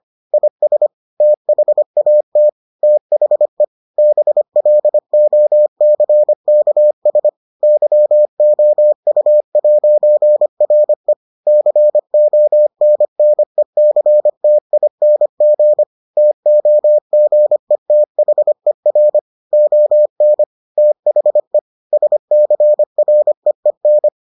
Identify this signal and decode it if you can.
Morse code, heard as a keyed tone, about 25 wpm, 600 Hz. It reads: IS THAT THE BLOCKS YOU'RE CONNECTING TOGETHER ON THE SCREEN